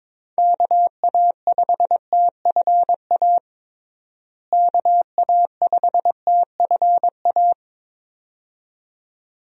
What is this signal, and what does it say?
Morse code, heard as a keyed tone, about 22 wpm, 700 Hz. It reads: KA5TFA KA5TFA
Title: USA Call Signs - 6 Characters - Repeated 2x - 22wpm